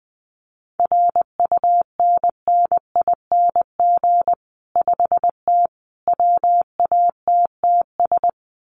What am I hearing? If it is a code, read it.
Morse code, heard as a keyed tone, about 20 wpm, 700 Hz. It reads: RUNNING 5T WATTS
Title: Common QSO Phrases 20wpm